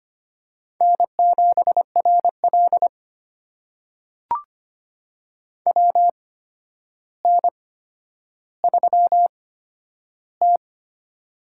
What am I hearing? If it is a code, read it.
Morse code, heard as a keyed tone, about 25 wpm, spaced as for 11 wpm, 700 Hz. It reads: N7RL WN3T
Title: USA Call Signs - 4 Characters - 8x Character Spacing - 25wpm